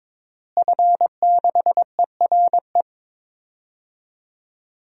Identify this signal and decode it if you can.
Morse code, heard as a keyed tone, about 22 wpm, 700 Hz. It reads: F6ERE
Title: DXCC-100 Call Signs - 5 Characters - Repeated 3x - 22wpm